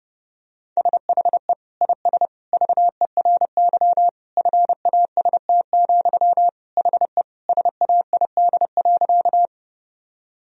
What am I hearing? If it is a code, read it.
Morse code, heard as a keyed tone, about 32 wpm, 700 Hz. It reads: SHE IS VERY FAST, HE SAID.
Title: Sentences from Top 1000 Words - Review Part 2 - 30wpm